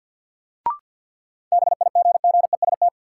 Morse code, heard as a keyed tone, about 46 wpm, 700 Hz. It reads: BIGGEST
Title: Sets of 4 Words - New Word Drill - 50wpm